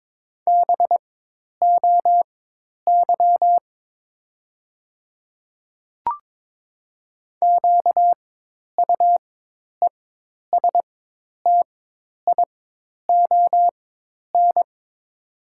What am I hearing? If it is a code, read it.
Morse code, heard as a keyed tone, about 22 wpm, spaced as for 10 wpm, 700 Hz. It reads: BOY QUESTION
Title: 4x Character Spacing - Top 500 Words - 22wpm